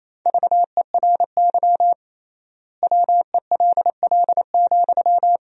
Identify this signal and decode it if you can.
Morse code, heard as a keyed tone, about 28 wpm, 700 Hz. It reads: VERY WELL,
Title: 3x Word-Spacing - Sentences from Top 200 Words - 28wpm